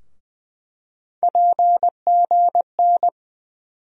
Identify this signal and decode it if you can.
Morse code, heard as a keyed tone, about 20 wpm, 700 Hz. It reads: PGN
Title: USA Call Sign Postfixes - 3 Letters 20wpm